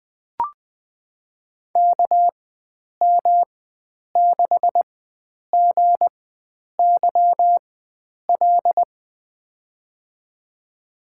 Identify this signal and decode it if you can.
Morse code, heard as a keyed tone, about 20 wpm, spaced as for 9 wpm, 700 Hz. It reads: KM6GYL
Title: USA Call Signs - 4x Character Spacing - 20wpm